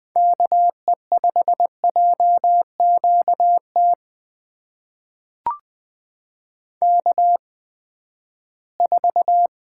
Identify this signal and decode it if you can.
Morse code, heard as a keyed tone, about 20 wpm, spaced as for 14 wpm, 700 Hz. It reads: KE5JQT K4
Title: USA Call Signs - 8x Character Spacing - 20wpm